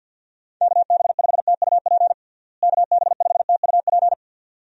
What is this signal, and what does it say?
Morse code, heard as a keyed tone, about 47 wpm, 700 Hz. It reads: KB5TUP KB5TUP
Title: USA Call Signs - 6 Characters - Repeated 2x - 50wpm